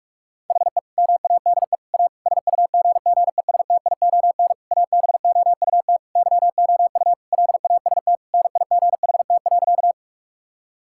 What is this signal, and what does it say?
Morse code, heard as a keyed tone, about 46 wpm, 700 Hz. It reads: HE MADE A SUGGESTION ABOUT YOU LAST NIGHT.
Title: Sentences from Top 1600 Words 45wpm